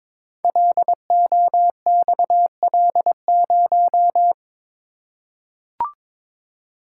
Morse code, heard as a keyed tone, about 22 wpm, 700 Hz. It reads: LOXL0
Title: Code Groups - 5 Letter:Numbers - 22wpm